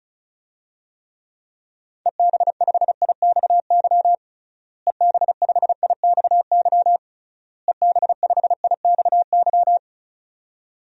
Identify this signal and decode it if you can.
Morse code, heard as a keyed tone, about 34 wpm, 700 Hz. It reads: EB5IXY EB5IXY EB5IXY
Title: DXCC-100 Call Signs - 6 Characters - Repeated 3x - 35wpm